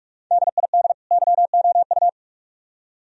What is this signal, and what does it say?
Morse code, heard as a keyed tone, about 46 wpm, 700 Hz. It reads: DID YOU